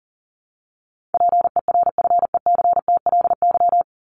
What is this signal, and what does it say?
Morse code, heard as a keyed tone, about 40 wpm, 700 Hz. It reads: PERFECTLY
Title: Top 1501-1750 Words 40wpm